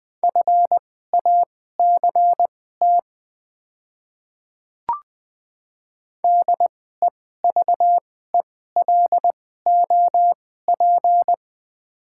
Morse code, heard as a keyed tone, about 20 wpm, spaced as for 15 wpm, 690 Hz. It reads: FACT DEVELOP
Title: 2x Character Spacing - Top 500 Words - 20wpm